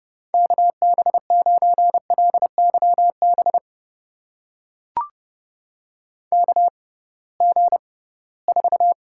Morse code, heard as a keyed tone, about 32 wpm, spaced as for 20 wpm, 700 Hz. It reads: KB9LYB KG4
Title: USA Call Signs - 6 Characters - 6x Character Spacing - 30wpm